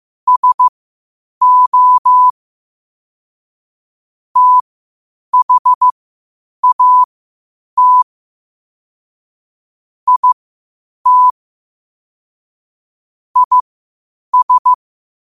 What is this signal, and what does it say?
Morse code, heard as a keyed tone, about 15 wpm, spaced as for 8 wpm, 1000 Hz. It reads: SO THAT IT IS
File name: qotdfiles / 8wpm